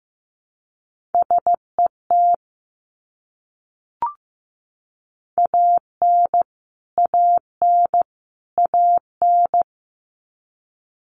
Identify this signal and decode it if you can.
Morse code, heard as a keyed tone, about 15 wpm, 700 Hz. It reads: SET AN AN AN
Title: Top 200 Words Repeated 15wpm